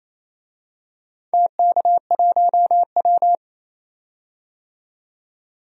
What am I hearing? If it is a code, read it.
Morse code, heard as a keyed tone, about 27 wpm, 690 Hz. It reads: TK1W
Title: DXCC-100 Call Signs - 4 Characters - 28wpm